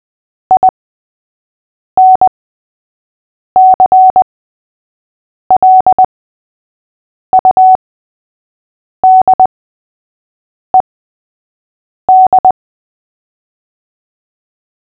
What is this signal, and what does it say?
Morse code, heard as a keyed tone, about 20 wpm, spaced as for 6 wpm, 750 Hz. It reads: INCLUDED
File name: cw / fists / Short-Story-3-6wpm